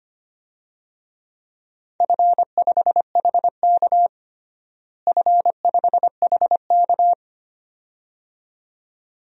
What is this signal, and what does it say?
Morse code, heard as a keyed tone, about 25 wpm, 700 Hz. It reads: F5HK F5HK
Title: DXCC-100 Call Signs - 4 Characters - Repeated 2x - 25wpm